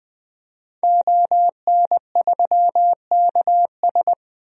Morse code, heard as a keyed tone, about 20 wpm, 700 Hz. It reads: ON3KS
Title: DXCC-100 Call Signs - 5 Characters - 20wpm